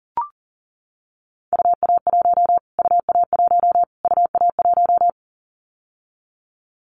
Morse code, heard as a keyed tone, about 40 wpm, 700 Hz. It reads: UA1 UA1 UA1
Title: Callsign Prefixes for Top 25 DXCC - Repeated 3x - 40wpm